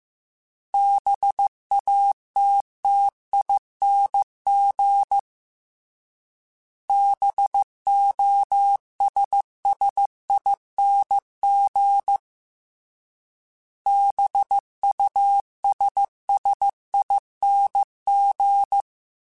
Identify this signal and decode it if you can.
Morse code, heard as a keyed tone, about 15 wpm, 800 Hz. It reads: BATTING BOSSING BUSSING
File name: Special INGs-mp3s / Special INGs-mp3s / Sngl ing @13